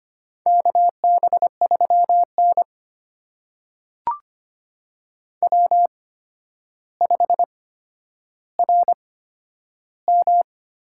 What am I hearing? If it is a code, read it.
Morse code, heard as a keyed tone, about 25 wpm, spaced as for 11 wpm, 700 Hz. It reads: KB3N W5RM
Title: USA Call Signs - 4 Characters - 8x Character Spacing - 25wpm